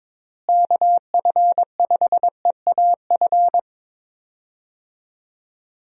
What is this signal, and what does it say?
Morse code, heard as a keyed tone, about 22 wpm, 700 Hz. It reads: KF5EAF
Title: USA Call Signs - 6 Characters - Repeated 2x - 22wpm